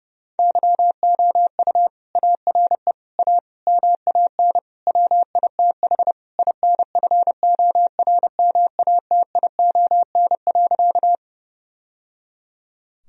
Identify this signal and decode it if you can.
Morse code, heard as a keyed tone, about 30 wpm, 700 Hz. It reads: YOU ARE A MAN WITH INFORMATION.
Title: Sentences from Top 1000 Words - Review Part 3 - 30wpm